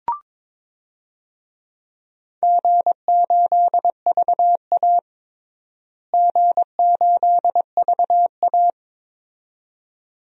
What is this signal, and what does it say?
Morse code, heard as a keyed tone, about 22 wpm, 700 Hz. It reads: G8VA G8VA
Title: DXCC-100 Call Signs - 4 Characters - Repeated 2x - 22wpm